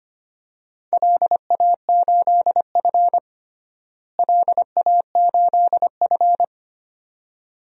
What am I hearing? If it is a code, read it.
Morse code, heard as a keyed tone, about 25 wpm, 700 Hz. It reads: LA8F LA8F
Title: DXCC-100 Call Signs - 4 Characters - Repeated 2x - 25wpm